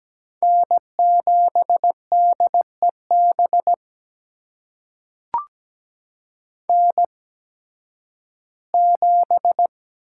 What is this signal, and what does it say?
Morse code, heard as a keyed tone, about 17 wpm, spaced as for 11 wpm, 700 Hz. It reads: N7DEB N7
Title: USA Call Signs - 5 Characters - 8x Character Spacing - 17wpm